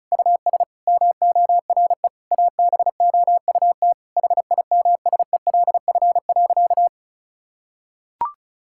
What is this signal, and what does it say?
Morse code, heard as a keyed tone, about 36 wpm, 700 Hz. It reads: US MORE ABOUT HIMSELF.
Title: Sentences from Top 1600 Words 35wpm